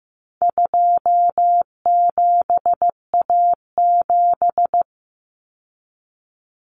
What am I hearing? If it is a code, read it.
Morse code, heard as a keyed tone, about 15 wpm, 700 Hz. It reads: 27A7
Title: Common Cut Numbers - 4-Digit 15wpm